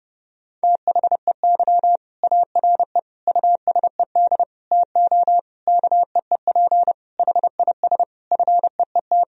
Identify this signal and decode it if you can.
Morse code, heard as a keyed tone, about 30 wpm, 700 Hz. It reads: THEY ARE USED TO KEEP HIS FEET
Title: Sentences from Top 1000 Words - Review Part 3 - 30wpm